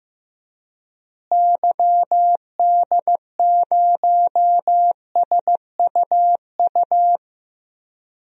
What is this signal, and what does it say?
Morse code, heard as a keyed tone, about 15 wpm, 700 Hz. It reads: YD0SUU
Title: DXCC-100 Call Signs - 6 Characters - Repeated 2x - 15wpm